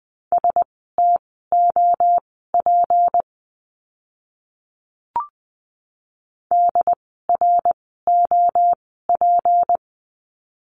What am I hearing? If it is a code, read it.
Morse code, heard as a keyed tone, about 20 wpm, spaced as for 14 wpm, 700 Hz. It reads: STOP DROP
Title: 153 Lesson 25f - Letter p - Words 20wpm